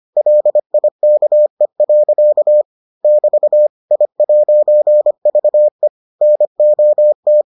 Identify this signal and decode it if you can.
Morse code, heard as a keyed tone, about 25 wpm, 600 Hz. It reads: LIKE. <BT> I'VE NOT